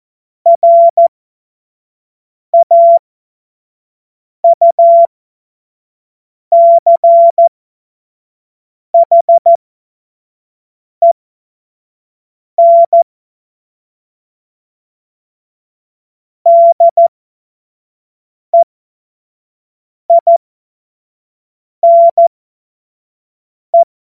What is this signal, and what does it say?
Morse code, heard as a keyed tone, about 14 wpm, spaced as for 5 wpm, 680 Hz. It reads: RAUCHEN DEINE
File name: Witz-26-August-2025-25-F70